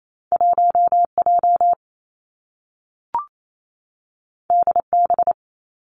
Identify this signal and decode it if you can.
Morse code, heard as a keyed tone, about 28 wpm, 700 Hz. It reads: 1 J D B
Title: Two Challenging Letter-Number Combinations 28wpm